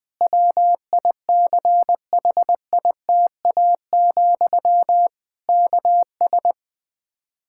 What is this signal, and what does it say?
Morse code, heard as a keyed tone, about 21 wpm, 700 Hz. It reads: WICHITA, KS
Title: Practice QSO - Exchange 1 - Beginning - 20wpm